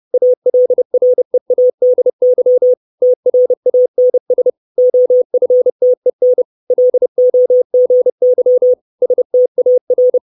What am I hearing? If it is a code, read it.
Morse code, heard as a keyed tone, about 30 wpm, 500 Hz. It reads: ALREADY TRANS OFTEN LOGY STAR
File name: 632 Words 30WPM